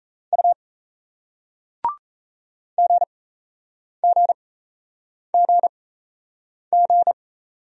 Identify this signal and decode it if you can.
Morse code, heard as a keyed tone, about 36 wpm, spaced as for 8 wpm, 700 Hz. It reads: U GGGG